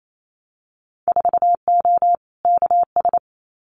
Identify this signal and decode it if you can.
Morse code, heard as a keyed tone, about 28 wpm, 700 Hz. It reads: 4O KS
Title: Winter Field Day Exchanges Encore 28wpm